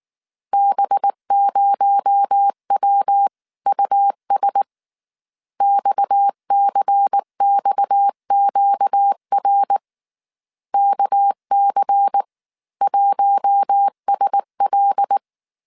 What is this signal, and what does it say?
Morse code, heard as a keyed tone, about 19 wpm, 800 Hz. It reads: B0WUS XCXQR KC1SL